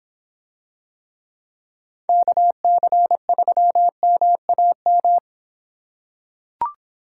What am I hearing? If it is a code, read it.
Morse code, heard as a keyed tone, about 26 wpm, 700 Hz. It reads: KC3MAM